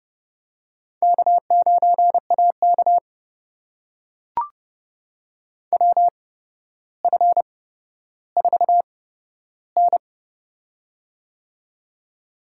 Morse code, heard as a keyed tone, about 30 wpm, spaced as for 13 wpm, 700 Hz. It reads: K9AK WF4N
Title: USA Call Signs - 4 Characters - 8x Character Spacing - 30wpm